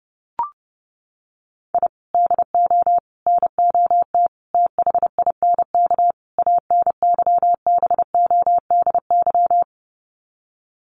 Morse code, heard as a keyed tone, about 30 wpm, 700 Hz. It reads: I DO NOT THINK ANYBODY
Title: Sets of 5 Words 30wpm